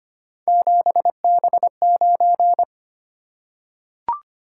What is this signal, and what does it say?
Morse code, heard as a keyed tone, about 25 wpm, 700 Hz. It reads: 7B9
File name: Three Letter-Number Combinations - 25wpm